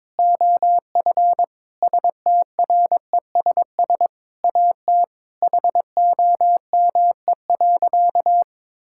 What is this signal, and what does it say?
Morse code, heard as a keyed tone, about 22 wpm, 700 Hz. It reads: OF STRESS AT HOME.
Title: Sentences from Top 1000 Words - Review Part 6 - 22wpm